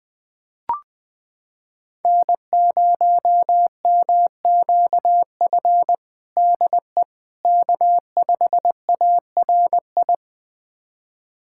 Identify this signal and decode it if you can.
Morse code, heard as a keyed tone, about 20 wpm, 700 Hz. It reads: N0MQF DE K5ARI
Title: USA Callsign DE Callsign 20wpm